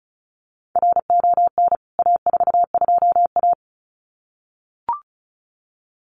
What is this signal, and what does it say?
Morse code, heard as a keyed tone, about 34 wpm, 700 Hz. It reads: RON A42A
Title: CWT Exchange 35wpm